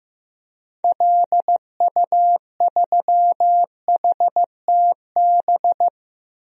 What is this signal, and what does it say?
Morse code, heard as a keyed tone, about 15 wpm, 700 Hz. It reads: LU3HTB
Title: DXCC-100 Call Signs - 6 Characters - Repeated 3x - 15wpm